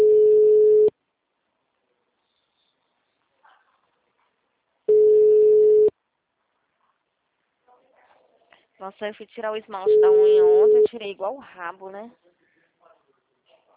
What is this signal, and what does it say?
Ringback tone (Europe, 425 Hz), with speech.
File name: out- -2015- - - .8323